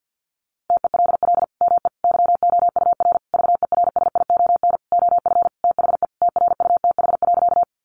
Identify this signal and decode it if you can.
Morse code, heard as a keyed tone, about 48 wpm, 700 Hz. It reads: TELL ME YOUR VERSION OF THE TRUTH.
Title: Sentences from Top 1400 Words 50wpm